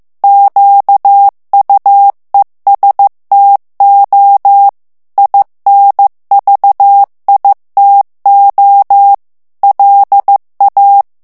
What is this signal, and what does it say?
Morse code, heard as a keyed tone, about 15 wpm, 800 Hz. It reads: QUESTO INVITO LA
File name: ITALIEN texte clair 1 vitesse 14 mots minute